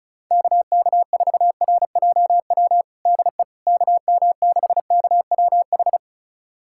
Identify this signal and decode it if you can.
Morse code, heard as a keyed tone, about 36 wpm, 700 Hz. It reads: KK4RJW DE KM6KWH
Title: USA Callsign DE Callsign 35wpm